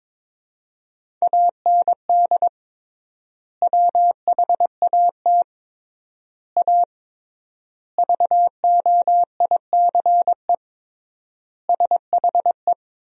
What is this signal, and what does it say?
Morse code, heard as a keyed tone, about 22 wpm, 700 Hz. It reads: AND WHAT A VOICE SHE
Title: 3x Word-Spacing - Sentences from Top 500 Words - 22wpm